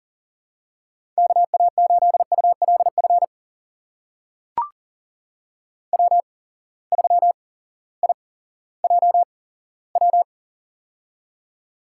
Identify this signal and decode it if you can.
Morse code, heard as a keyed tone, about 43 wpm, spaced as for 19 wpm, 690 Hz. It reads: KA8ULF W3IJW
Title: USA Call Signs - 8x Character Spacing - 40wpm